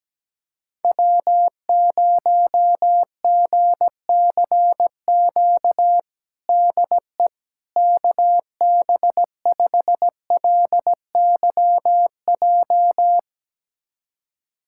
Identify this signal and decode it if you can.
Morse code, heard as a keyed tone, about 17 wpm, 700 Hz. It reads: W0GCQ DE KB5LYJ
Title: USA Callsign DE Callsign 17wpm